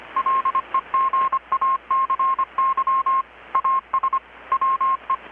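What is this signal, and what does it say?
Morse code, heard as a keyed tone, about 25 wpm, 1070 Hz. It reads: LEGACY AS WE
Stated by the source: Morse from KPH's Night of Nights XX maritime radiotelegraphy event, received in Chiapas, Mexico about 3,800 km away on a Kenwood R-600 with a grounded 10 m long-wire antenna.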